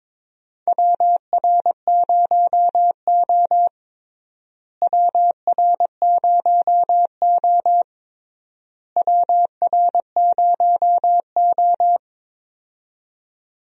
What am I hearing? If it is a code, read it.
Morse code, heard as a keyed tone, about 22 wpm, 700 Hz. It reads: WR0O WR0O WR0O
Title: USA Call Signs - 4 Characters - Repeated 3x - 22wpm